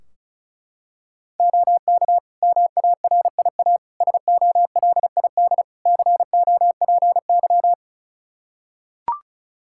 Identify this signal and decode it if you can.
Morse code, heard as a keyed tone, about 34 wpm, 700 Hz. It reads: OK MARIA SOLID COPY
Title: Practice QSO - Exchange 3 - More Info - 35wpm